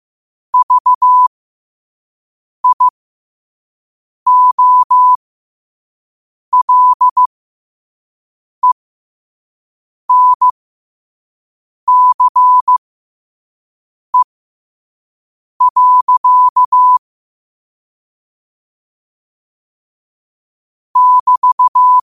Morse code, heard as a keyed tone, about 15 wpm, spaced as for 5 wpm, 1000 Hz. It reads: VIOLENCE. <BT>
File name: qotdfiles / 5wpm